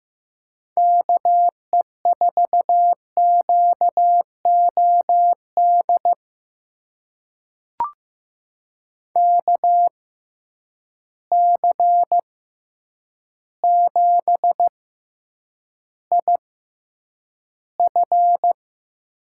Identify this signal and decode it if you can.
Morse code, heard as a keyed tone, about 15 wpm, spaced as for 8 wpm, 700 Hz. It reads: KE4QOD KC7IF
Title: USA Call Signs - 6 Characters - 6x Character Spacing - 15wpm